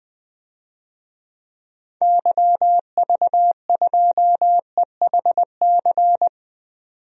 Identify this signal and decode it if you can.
Morse code, heard as a keyed tone, about 20 wpm, 700 Hz. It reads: YV2EHC
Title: DXCC-100 Call Signs - 6 Characters - 20wpm